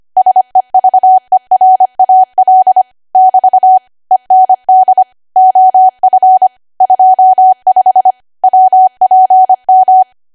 Morse code, heard as a keyed tone, about 25 wpm, 750 Hz. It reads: SEVERAL <BT> END OF 25 WPM
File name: Archive / 25 WPM / 25WPM